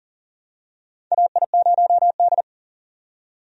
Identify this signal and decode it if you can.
Morse code, heard as a keyed tone, about 40 wpm, 700 Hz. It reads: AI0D
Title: USA Call Signs - 4 Characters - 40wpm